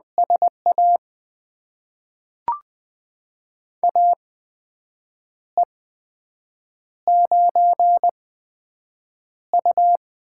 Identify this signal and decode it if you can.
Morse code, heard as a keyed tone, about 20 wpm, spaced as for 7 wpm, 700 Hz. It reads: SA AE9U
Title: USA Call Signs - 8x Character Spacing - 20wpm